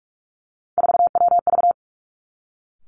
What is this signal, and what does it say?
Morse code, heard as a keyed tone, about 46 wpm, 700 Hz. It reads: 4WV
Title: Three Challenging Letter-Number Combinations 45wpm